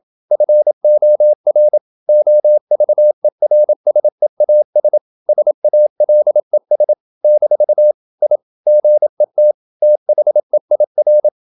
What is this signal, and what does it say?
Morse code, heard as a keyed tone, about 28 wpm, 600 Hz. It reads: FOR OVERSEAS SALES - I GET THEIR
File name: Never 0027wpm